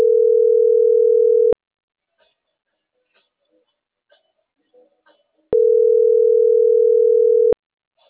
Ringback tone (North America), 440 + 480 Hz.